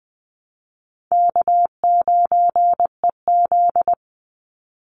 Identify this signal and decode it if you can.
Morse code, heard as a keyed tone, about 20 wpm, 700 Hz. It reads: K9EZ
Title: NA SOTA Chaser Callsigns 20wpm